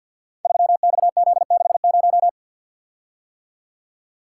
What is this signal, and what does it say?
Morse code, heard as a keyed tone, about 47 wpm, 700 Hz. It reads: 3XZ60
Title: Code Groups - 5 Letter:Numbers - 50wpm